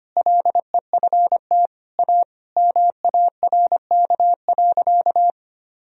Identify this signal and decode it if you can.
Morse code, heard as a keyed tone, about 25 wpm, 700 Hz. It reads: LEFT A MARK.
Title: Sentences from Top 1000 Words - Review Part 1 - 25wpm